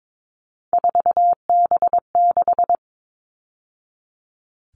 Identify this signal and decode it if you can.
Morse code, heard as a keyed tone, about 22 wpm, 700 Hz. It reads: 4B6
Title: All Cut Numbers - 3-Digit 22wpm